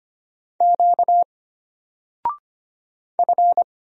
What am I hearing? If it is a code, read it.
Morse code, heard as a keyed tone, about 25 wpm, 700 Hz. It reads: Q F
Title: Single Letter-Number Alt 25wpm